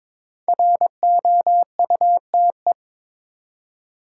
Morse code, heard as a keyed tone, about 22 wpm, 700 Hz. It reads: ROUTE
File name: practice / 22wpm / 5 Letter Words - First 3 Letters Given - 22wpm